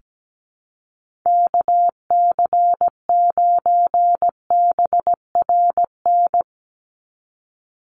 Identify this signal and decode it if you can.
Morse code, heard as a keyed tone, about 17 wpm, 700 Hz. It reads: KC9BRN
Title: USA Call Signs 17wpm